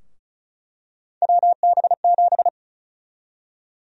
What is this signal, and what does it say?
Morse code, heard as a keyed tone, about 36 wpm, 700 Hz. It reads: WB7
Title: USA Call Sign Prefixes - 35wpm